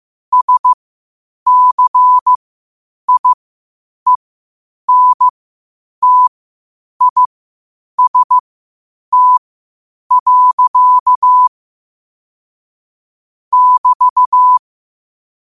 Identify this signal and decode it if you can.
Morse code, heard as a keyed tone, about 15 wpm, spaced as for 8 wpm, 1000 Hz. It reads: SCIENTIST. <BT>
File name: qotdfiles / 8wpm